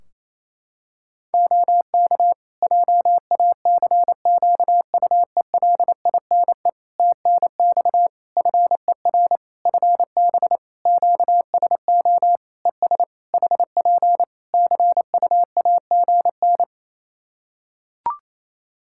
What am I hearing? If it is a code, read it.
Morse code, heard as a keyed tone, about 28 wpm, 700 Hz. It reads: OK JACQUELINE TNX FER FB QSO ES HP CUAGN
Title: K1USN SST QSO Practice 28wpm